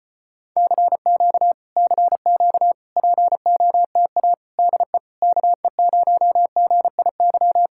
Morse code, heard as a keyed tone, about 36 wpm, 700 Hz. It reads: CQ CQ POTA DE KE0GIY